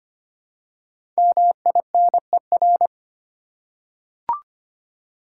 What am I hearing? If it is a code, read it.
Morse code, heard as a keyed tone, about 25 wpm, 700 Hz. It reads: MINER
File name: 5 Letter Words - First 3 Letters Given - 25wpm